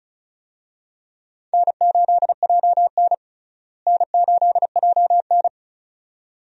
Morse code, heard as a keyed tone, about 36 wpm, 700 Hz. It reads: N8JN N8JN
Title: USA Call Signs - 4 Characters - Repeated 2x - 35wpm